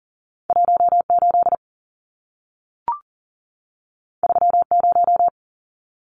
Morse code, heard as a keyed tone, about 40 wpm, 700 Hz. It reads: 18 30
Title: Numbers - 2-Digit 40wpm